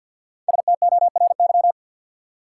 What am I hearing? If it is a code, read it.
Morse code, heard as a keyed tone, about 53 wpm, 700 Hz. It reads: STORY
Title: Five Letter Words - 50wpm